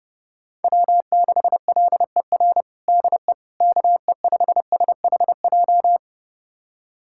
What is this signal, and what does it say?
Morse code, heard as a keyed tone, about 30 wpm, 700 Hz. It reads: W6LER DE KE5SHJ
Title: USA Callsign DE Callsign 30wpm